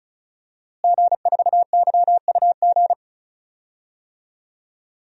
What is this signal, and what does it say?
Morse code, heard as a keyed tone, about 36 wpm, 700 Hz. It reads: G4YUG
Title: DXCC-100 Call Signs - 5 Characters - Repeated 3x - 35wpm